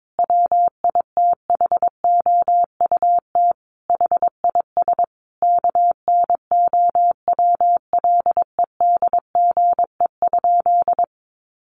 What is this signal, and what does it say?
Morse code, heard as a keyed tone, about 22 wpm, 700 Hz. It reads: WITHOUT HIS KNOWLEDGE?
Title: Sentences from Top 600 Words 22wpm